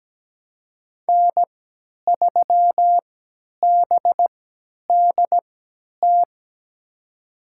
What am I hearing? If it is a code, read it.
Morse code, heard as a keyed tone, about 17 wpm, spaced as for 10 wpm, 700 Hz. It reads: N3BDT
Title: USA Call Signs - 3x Character Spacing - 17wpm